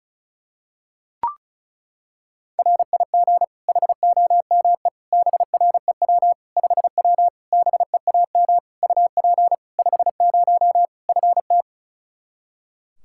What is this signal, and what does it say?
Morse code, heard as a keyed tone, about 34 wpm, 700 Hz. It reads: RIG HOME BREW 5W BEAM UP 50 FT
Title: Practice QSO - Exchange 2 - Info - 35wpm